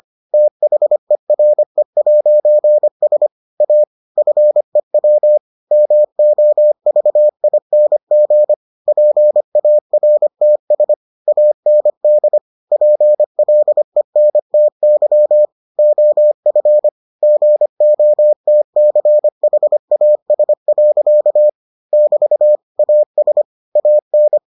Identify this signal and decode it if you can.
Morse code, heard as a keyed tone, about 25 wpm, 600 Hz. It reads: THERE'S A FEW MOVING PARTS AND PLENTY OF GOTCHAS. <BT> AS AN